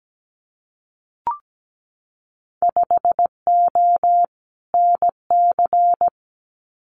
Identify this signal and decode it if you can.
Morse code, heard as a keyed tone, about 17 wpm, 700 Hz. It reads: 5O NC
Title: Winter Field Day Exchanges 17wpm